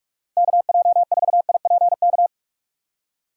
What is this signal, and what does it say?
Morse code, heard as a keyed tone, about 46 wpm, 700 Hz. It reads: KJ4IPK
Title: USA Call Signs - 4x Character Spacing - 45wpm